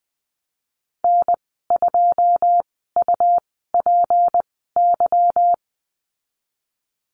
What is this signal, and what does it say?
Morse code, heard as a keyed tone, about 20 wpm, spaced as for 15 wpm, 700 Hz. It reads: N2UPY
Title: 161 Lesson 26g - Letter g - Call Signs 20wpm